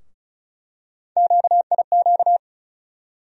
Morse code, heard as a keyed tone, about 34 wpm, 700 Hz. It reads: QIQ
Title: USA Call Sign Postfixes - 3 Letters 35wpm